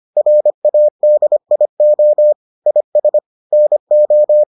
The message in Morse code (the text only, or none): RADIO IS NO